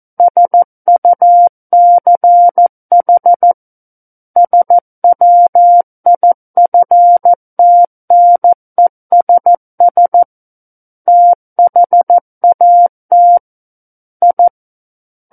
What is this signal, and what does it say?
Morse code, heard as a keyed tone, about 14 wpm, 700 Hz. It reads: SUCH SWIFTNESS THAT I
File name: Book One / 05-Chapter 5 - 14 WPM